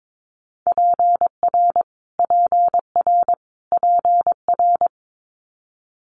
Morse code, heard as a keyed tone, about 22 wpm, 700 Hz. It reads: PR PR PR
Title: Multi-letter Phonemes Repeated 3x 22wpm